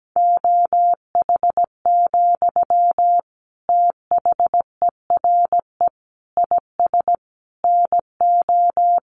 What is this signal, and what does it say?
Morse code, heard as a keyed tone, about 17 wpm, 700 Hz. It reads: OH, THERE IS NO
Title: Sentences from Top 1500 Words 17wpm